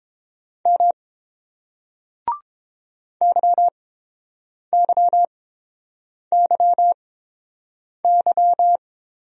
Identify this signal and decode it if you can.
Morse code, heard as a keyed tone, about 30 wpm, spaced as for 7 wpm, 700 Hz. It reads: M YYYY